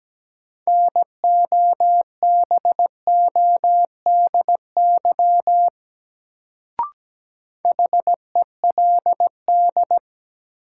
Morse code, heard as a keyed tone, about 17 wpm, 700 Hz. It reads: NOBODY HELD
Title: Sentences from Top 1600 Words - New Word Drill Alt - 17wpm